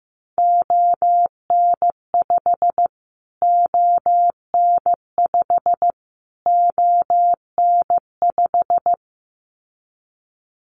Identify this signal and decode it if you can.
Morse code, heard as a keyed tone, about 15 wpm, 700 Hz. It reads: ON5 ON5 ON5
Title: Callsign Prefixes for Top 25 DXCC - Repeated 3x - 15wpm